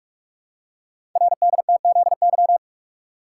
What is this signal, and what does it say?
Morse code, heard as a keyed tone, about 44 wpm, 690 Hz. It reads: RDTZY
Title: Code Groups - 5 Letters - 45wpm